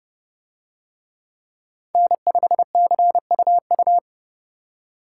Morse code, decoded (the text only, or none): N5CUU